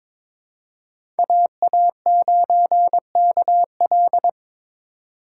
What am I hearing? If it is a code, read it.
Morse code, heard as a keyed tone, about 22 wpm, 700 Hz. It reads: AA9KL
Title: USA Call Signs - 5 Characters - 8x Character Spacing - 22wpm